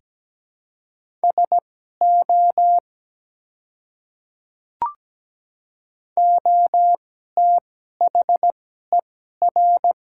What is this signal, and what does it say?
Morse code, heard as a keyed tone, about 17 wpm, spaced as for 12 wpm, 700 Hz. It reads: SO OTHER
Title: 2x Character Spacing - Top 100 Words - 17wpm